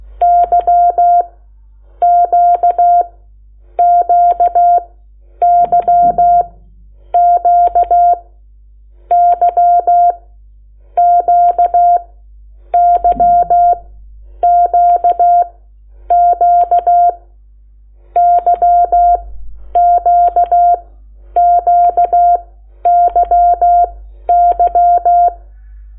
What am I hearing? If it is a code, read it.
Morse code, heard as a keyed tone, about 16 wpm, spaced as for 9 wpm, 700 Hz. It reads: YQQYQ YQYQQ YQQYY
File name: Lecciones / CLASE Nº8